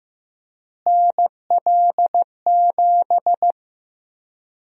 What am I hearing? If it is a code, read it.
Morse code, heard as a keyed tone, about 15 wpm, 700 Hz. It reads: NL7
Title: USA Call Sign Prefixes - Repeated 2x - 15wpm